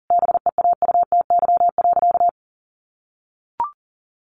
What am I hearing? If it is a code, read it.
Morse code, heard as a keyed tone, about 40 wpm, 700 Hz. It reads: BEAUTY.
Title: Sentences from Top 1100 Words 40wpm